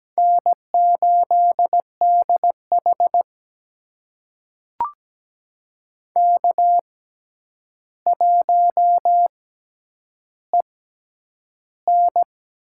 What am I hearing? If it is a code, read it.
Morse code, heard as a keyed tone, about 17 wpm, spaced as for 9 wpm, 700 Hz. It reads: N8DH K1EN
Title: USA Call Signs - 6x Character Spacing - 17wpm